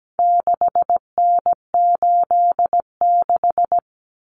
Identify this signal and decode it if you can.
Morse code, heard as a keyed tone, about 17 wpm, 700 Hz. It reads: 6N86
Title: Common Cut Numbers - 4-Digit 17wpm